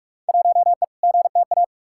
Morse code, heard as a keyed tone, about 45 wpm, 700 Hz. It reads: 1E GTA
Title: Field Day Exchanges - 45wpm